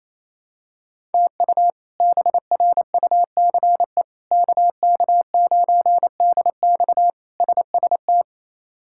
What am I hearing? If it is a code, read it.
Morse code, heard as a keyed tone, about 28 wpm, 700 Hz. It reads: TU BRUCE KK9DX SST
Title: K1USN SST QSO Practice - 28wpm